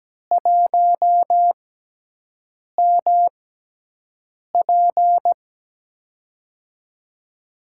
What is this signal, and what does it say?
Morse code, heard as a keyed tone, about 17 wpm, spaced as for 6 wpm, 700 Hz. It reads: 1MP
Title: USA Call Signs - 5 Characters - 6x Character Spacing - 17wpm